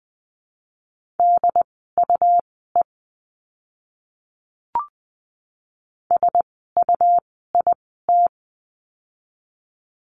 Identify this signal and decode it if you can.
Morse code, heard as a keyed tone, about 20 wpm, spaced as for 15 wpm, 700 Hz. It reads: DUE SUIT
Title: 101 Lesson 17f - Letter u - Words 20wpm